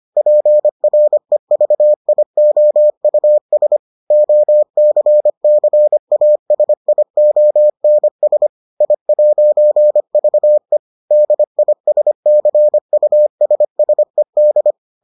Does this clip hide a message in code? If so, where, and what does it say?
Yes: Morse code from 0.2 to 14.7 s, heard as a keyed tone, about 25 wpm, 600 Hz. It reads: PREVIOUS OCCASIONS I'VE DISCUSSED